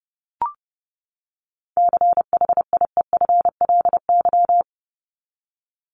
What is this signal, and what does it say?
Morse code, heard as a keyed tone, about 30 wpm, 700 Hz. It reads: CHIEFLY
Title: LY Suffix - Top 100 Words with Base Words 30wpm